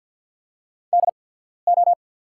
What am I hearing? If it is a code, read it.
Morse code, heard as a keyed tone, about 48 wpm, spaced as for 13 wpm, 700 Hz. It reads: DO
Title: 8x Character Spacing - Top 200 Words - 50wpm